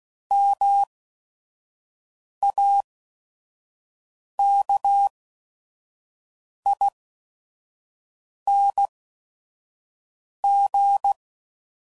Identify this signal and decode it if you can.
Morse code, heard as a keyed tone, about 16 wpm, spaced as for 5 wpm, 800 Hz. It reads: MAKING
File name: Suffixes / Special ING's / VerySlowToFaster ING's